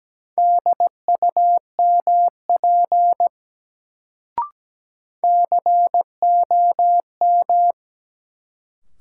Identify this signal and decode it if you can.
Morse code, heard as a keyed tone, about 17 wpm, 700 Hz. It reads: DUMP COM
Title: Sentences from Top 1300 Words - New Word Drill - 17wpm